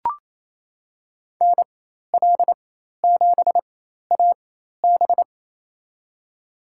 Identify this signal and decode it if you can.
Morse code, heard as a keyed tone, about 28 wpm, spaced as for 13 wpm, 700 Hz. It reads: NL7AB
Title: USA Call Signs - 4x Character Spacing - 28wpm